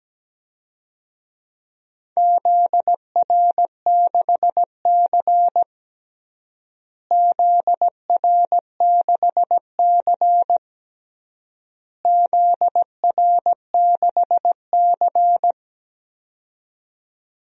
Morse code, heard as a keyed tone, about 17 wpm, 700 Hz. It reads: ZR6C ZR6C ZR6C
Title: DXCC-100 Call Signs - 4 Characters - Repeated 3x - 17wpm